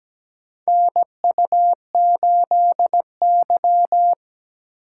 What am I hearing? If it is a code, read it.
Morse code, heard as a keyed tone, about 17 wpm, 700 Hz. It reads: NU8Y
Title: USA Call Signs - 4 Characters - 3x Character Spacing - 17wpm